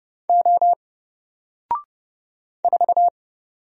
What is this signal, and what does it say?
Morse code, heard as a keyed tone, about 30 wpm, 700 Hz. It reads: O 4
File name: Single-Letter-Number-30wpm